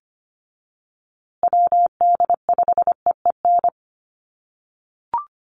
Morse code, heard as a keyed tone, about 25 wpm, 700 Hz. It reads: WD5EEN
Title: USA Call Signs Repeated 3x 25wpm